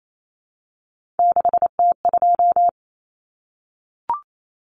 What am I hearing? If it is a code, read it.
Morse code, heard as a keyed tone, about 28 wpm, 700 Hz. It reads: 6T2
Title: Common Cut Numbers - 3-Digit 28wpm